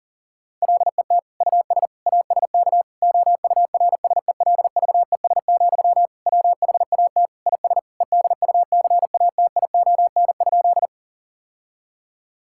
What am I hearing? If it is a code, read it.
Morse code, heard as a keyed tone, about 40 wpm, 700 Hz. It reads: LET US ASK OURSELVES, WHAT IS EDUCATION?
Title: Sentences from Top 1600 Words 40wpm